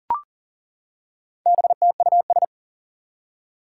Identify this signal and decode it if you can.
Morse code, heard as a keyed tone, about 40 wpm, 700 Hz. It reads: BTUS
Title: Four Challenging Letter-Number Combinations 40wpm